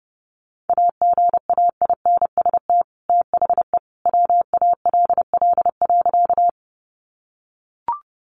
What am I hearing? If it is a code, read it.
Morse code, heard as a keyed tone, about 30 wpm, 700 Hz. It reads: AGAINST THE WALL.
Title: Sentences from Top 1100 Words 30wpm